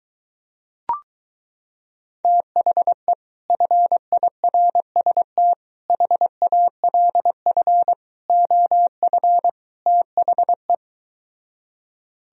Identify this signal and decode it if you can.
Morse code, heard as a keyed tone, about 23 wpm, 700 Hz. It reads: THE FIRST HALF OF THE